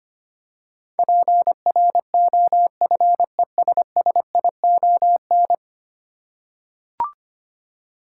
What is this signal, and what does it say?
Morse code, heard as a keyed tone, about 25 wpm, 700 Hz. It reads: PROFESSION
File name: 10 Letter Words - First 3 Letters Given - 25wpm